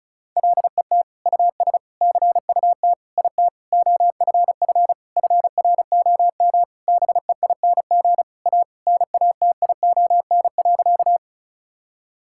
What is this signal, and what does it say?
Morse code, heard as a keyed tone, about 34 wpm, 700 Hz. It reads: LET US CUT IT OFF FROM BEING A NATION.
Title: Sentences from Top 1000 Words - Review Part 6 - 35wpm